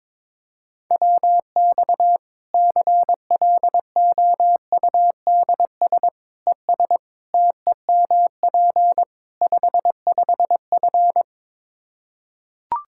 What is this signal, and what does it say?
Morse code, heard as a keyed tone, about 22 wpm, 700 Hz. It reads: WX CLOUDS ES TEMP 55F
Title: K1USN SST QSO Practice 22wpm